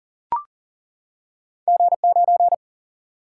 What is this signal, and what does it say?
Morse code, heard as a keyed tone, about 40 wpm, 700 Hz. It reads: G9
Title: Single-Letter Single-Number 40wpm